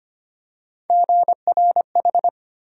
Morse code, heard as a keyed tone, about 25 wpm, 700 Hz. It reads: GRH
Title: USA Call Sign Postfixes - 3 Letters 25wpm